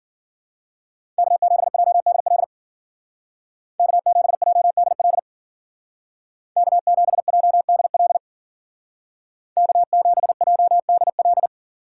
Morse code, heard as a keyed tone, about 49 wpm, 700 Hz. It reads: K7JDL K7JDL K7JDL K7JDL
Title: Speed Racing - USA 1x3 Call Signs 40wpm